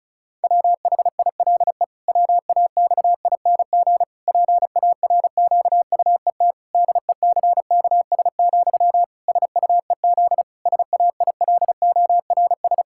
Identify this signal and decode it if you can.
Morse code, heard as a keyed tone, about 34 wpm, 700 Hz. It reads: WHILE WAXING PARQUET DECKS, SUEZ SAILORS
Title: Pangram Challenge v2 - 35wpm